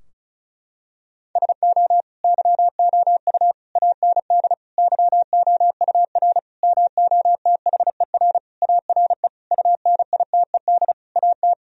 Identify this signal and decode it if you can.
Morse code, heard as a keyed tone, about 36 wpm, 700 Hz. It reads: SO YOU AND YOUR MOTHER ARE UNITED AT
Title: Sentences from Top 1000 Words - Review Part 4 - 35wpm